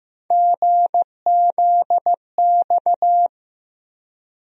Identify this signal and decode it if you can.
Morse code, heard as a keyed tone, about 15 wpm, 700 Hz. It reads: GZX
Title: USA Call Sign Postfixes - 3 Letters 15wpm